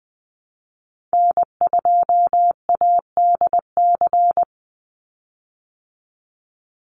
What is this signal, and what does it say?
Morse code, heard as a keyed tone, about 20 wpm, 700 Hz. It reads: N2ADC
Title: Call Signs Repeated 3x - 20wpm